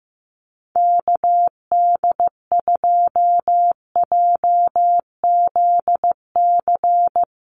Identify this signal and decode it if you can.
Morse code, heard as a keyed tone, about 15 wpm, 700 Hz. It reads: KD2JZC
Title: USA Call Signs Repeated 2x 15wpm